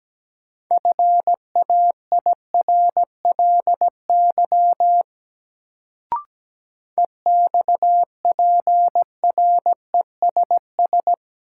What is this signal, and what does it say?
Morse code, heard as a keyed tone, about 17 wpm, 700 Hz. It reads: FAIRLY EXPRESS
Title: Sentences from Top 1100 Words - New Word Drill - 17wpm